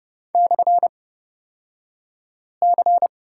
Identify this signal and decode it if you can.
Morse code, heard as a keyed tone, about 30 wpm, spaced as for 5 wpm, 700 Hz. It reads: /C
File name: Lesson 38d - Letter q - Two Characters 30wpm